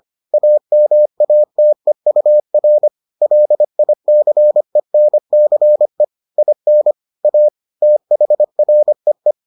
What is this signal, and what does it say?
Morse code, heard as a keyed tone, about 25 wpm, 600 Hz. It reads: AMATEUR LICENCE IN A THREE